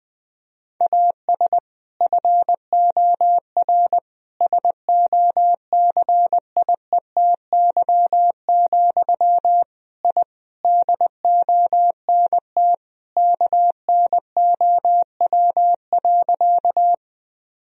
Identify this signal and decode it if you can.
Morse code, heard as a keyed tone, about 19 wpm, 700 Hz. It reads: AS FOR SOCIETY, I DONT KNOW.
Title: Sentences from Top 1000 Words - Review Part 4 - 20wpm